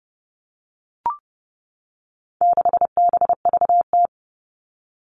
Morse code, heard as a keyed tone, about 28 wpm, 700 Hz. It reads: 6BVT
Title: All Cut Numbers - 4-Digit 30wpm